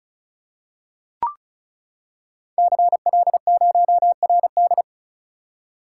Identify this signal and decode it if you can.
Morse code, heard as a keyed tone, about 36 wpm, 700 Hz. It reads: CL0RD
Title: Code Groups - 5 Letter:Numbers - 35wpm